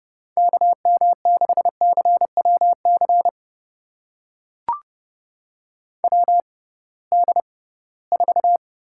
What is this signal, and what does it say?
Morse code, heard as a keyed tone, about 31 wpm, spaced as for 20 wpm, 700 Hz. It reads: KM6CWC WD4
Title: USA Call Signs - 6 Characters - 6x Character Spacing - 30wpm